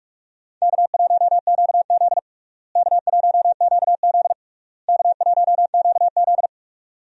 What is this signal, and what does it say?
Morse code, heard as a keyed tone, about 47 wpm, 700 Hz. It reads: K1QZ K1QZ K1QZ
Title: USA Call Signs - 4 Characters - Repeated 3x - 45wpm